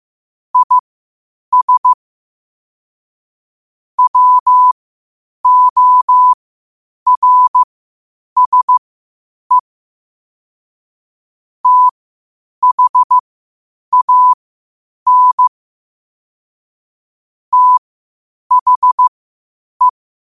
Morse code, heard as a keyed tone, about 15 wpm, spaced as for 8 wpm, 1000 Hz. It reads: IS WORSE THAN THE
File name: qotdfiles / 8wpm